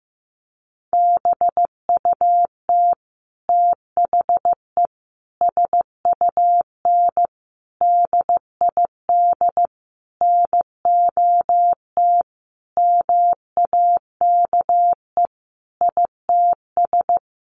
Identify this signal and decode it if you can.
Morse code, heard as a keyed tone, about 15 wpm, 700 Hz. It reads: BUT THE SUN DID NOT MAKE ITS
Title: Sentences from Top 1100 Words 15wpm